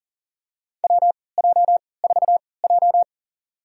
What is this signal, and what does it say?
Morse code, heard as a keyed tone, about 40 wpm, spaced as for 23 wpm, 700 Hz. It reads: WJ4J
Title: USA Call Signs - 4 Characters - 3x Character Spacing - 40wpm